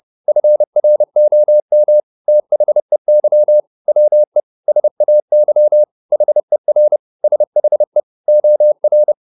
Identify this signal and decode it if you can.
Morse code, heard as a keyed tone, about 30 wpm, 600 Hz. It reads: FROM THEY WE SAY HER SHE OR
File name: Top-100-Words-30